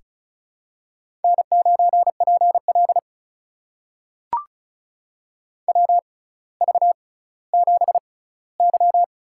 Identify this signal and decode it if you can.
Morse code, heard as a keyed tone, about 36 wpm, spaced as for 18 wpm, 700 Hz. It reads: N9PL WV7Y
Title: USA Call Signs - 4 Characters - 6x Character Spacing - 35wpm